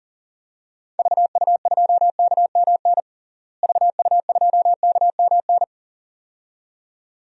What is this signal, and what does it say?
Morse code, heard as a keyed tone, about 40 wpm, 700 Hz. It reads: VU2KMN VU2KMN
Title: DXCC-100 Call Signs - 6 Characters - Repeated 2x - 40wpm